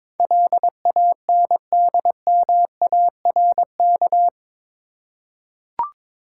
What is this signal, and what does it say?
Morse code, heard as a keyed tone, about 22 wpm, 700 Hz. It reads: LANDMARK
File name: practice / 22wpm / Eight Letter Words - 22wpm